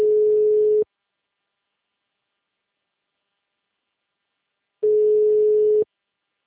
Ringback tone (Europe, 425 Hz).
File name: out- -2017- - - .209